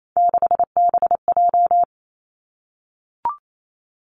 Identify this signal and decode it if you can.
Morse code, heard as a keyed tone, about 28 wpm, 700 Hz. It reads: 6BJ
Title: Three Challenging Letter-Number Combinations 28wpm